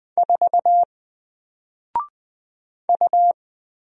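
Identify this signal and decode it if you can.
Morse code, heard as a keyed tone, about 20 wpm, 700 Hz. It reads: 4 U
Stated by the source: Single letter and number Morse drill at 20 wpm, from N4LCD.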